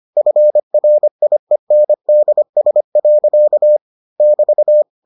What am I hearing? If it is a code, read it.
Morse code, heard as a keyed tone, about 25 wpm, 600 Hz. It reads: FRIENDS. <BT>